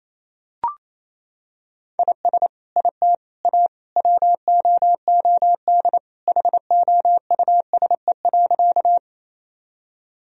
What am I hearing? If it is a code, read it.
Morse code, heard as a keyed tone, about 28 wpm, 700 Hz. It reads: IS IT A WOOD HOUSE.
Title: Sentences from Top 1000 Words - Review Part 2 - 28wpm